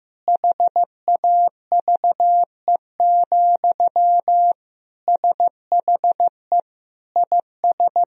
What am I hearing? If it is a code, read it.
Morse code, heard as a keyed tone, about 15 wpm, 700 Hz. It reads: HAVE, SHE IS
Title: Sentences from Top 1000 Words - Review Part 4 - 15wpm